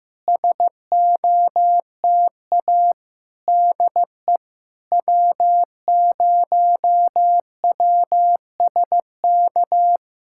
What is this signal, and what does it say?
Morse code, heard as a keyed tone, about 15 wpm, 700 Hz. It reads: SOTA DE W0WSK
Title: SOTA QSO Practice 15wpm